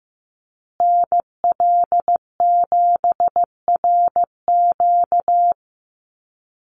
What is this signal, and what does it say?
Morse code, heard as a keyed tone, about 15 wpm, 700 Hz. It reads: NL7RQ
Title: USA Call Signs Repeated 3x 15wpm